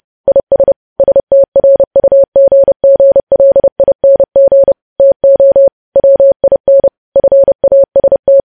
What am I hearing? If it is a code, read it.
Morse code, heard as a keyed tone, about 30 wpm, 550 Hz. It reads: IS STRUGGLING TO WIN FAST
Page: news headlines in morse code at 30 wpm